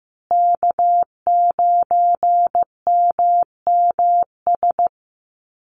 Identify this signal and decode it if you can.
Morse code, heard as a keyed tone, about 15 wpm, 700 Hz. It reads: K9MMS
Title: Call Signs Repeated 2x - 15wpm